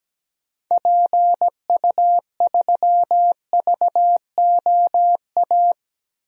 Morse code, heard as a keyed tone, about 17 wpm, 700 Hz. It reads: PU3VOA
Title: DXCC-100 Call Signs - 6 Characters - Repeated 3x - 17wpm